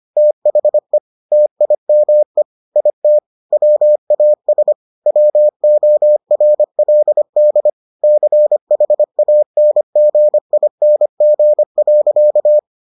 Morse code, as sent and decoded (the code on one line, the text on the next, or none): - .... / - .. -- . / .. - / .-- .- ... / .-- --- .-. .-.. -.. / -.-. .... .- -. --. .. -. --. .-.-.-
THE TIME IT WAS WORLD CHANGING.